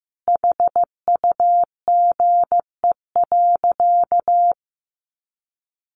Morse code, heard as a keyed tone, about 15 wpm, 700 Hz. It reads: HUGE.
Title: Sentences from Top 1100 Words 15wpm